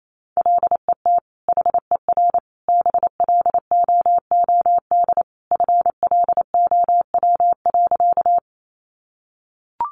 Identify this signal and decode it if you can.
Morse code, heard as a keyed tone, about 28 wpm, 700 Hz. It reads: LET HER BLOOD FLOW.
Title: Sentences from Top 1200 Words 28wpm